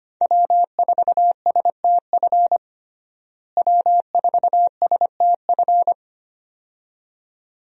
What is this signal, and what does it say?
Morse code, heard as a keyed tone, about 25 wpm, 700 Hz. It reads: W4STF W4STF
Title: USA Call Signs - 5 Characters - Repeated 2x - 25wpm